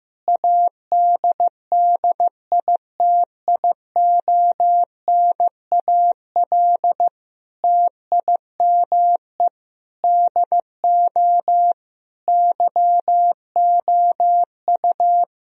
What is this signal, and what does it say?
Morse code, heard as a keyed tone, about 15 wpm, 700 Hz. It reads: ADDITIONAL TIME DO YOU
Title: Sentences from Top 1000 Words - Review Part 4 - 15wpm